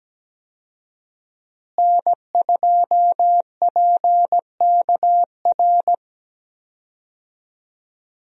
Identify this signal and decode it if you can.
Morse code, heard as a keyed tone, about 17 wpm, 700 Hz. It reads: N2PKR
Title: USA Call Signs - 5 Characters - Repeated 3x - 17wpm